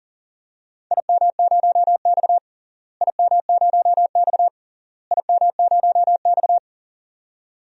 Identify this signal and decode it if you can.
Morse code, heard as a keyed tone, about 40 wpm, 700 Hz. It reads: IM0X IM0X IM0X
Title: DXCC-100 Call Signs - 4 Characters - Repeated 3x - 40wpm